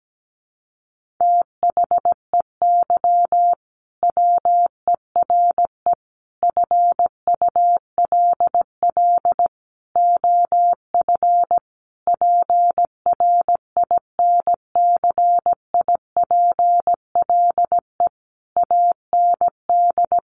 Morse code, heard as a keyed tone, about 17 wpm, 700 Hz. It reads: THEY WERE FULL OF PRINCIPLE AND
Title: Sentences from Top 1400 Words 17wpm